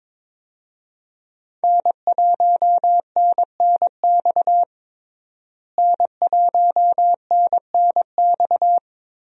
Morse code, heard as a keyed tone, about 22 wpm, 700 Hz. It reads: N1NNX N1NNX
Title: USA Call Signs - 5 Characters - Repeated 3x - 22wpm